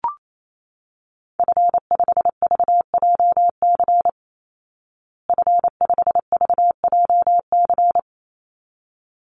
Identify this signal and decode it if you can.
Morse code, heard as a keyed tone, about 28 wpm, 700 Hz. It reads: F5VJC F5VJC
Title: Call Signs Repeated 3x - 28wpm